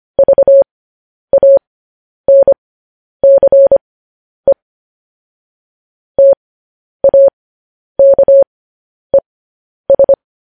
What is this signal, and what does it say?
Morse code, heard as a keyed tone, about 25 wpm, spaced as for 10 wpm, 550 Hz. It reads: VANCE TAKES
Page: news-headlines-in-morse-code-at-10-wpm